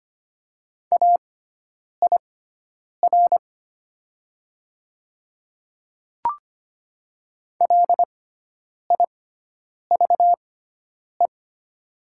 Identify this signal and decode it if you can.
Morse code, heard as a keyed tone, about 25 wpm, spaced as for 9 wpm, 700 Hz. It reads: AIR LIVE
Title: 6x Character Spacing - Top 200 Words - 25wpm